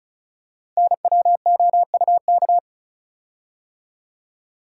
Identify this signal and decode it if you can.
Morse code, heard as a keyed tone, about 36 wpm, 700 Hz. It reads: NWOUK
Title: Code Groups - 5 Letters - 35wpm